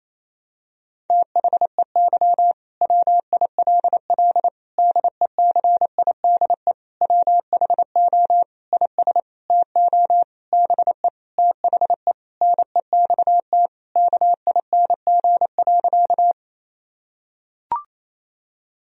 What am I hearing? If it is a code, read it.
Morse code, heard as a keyed tone, about 28 wpm, 700 Hz. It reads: THEY WILL DECIDE WHO IS TO BE THE NEXT KING.
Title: Sentences from Top 1000 Words - Review Part 3 - 28wpm